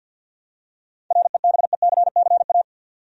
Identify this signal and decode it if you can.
Morse code, heard as a keyed tone, about 47 wpm, 700 Hz. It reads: REBECCA
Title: USA Common Names - 50wpm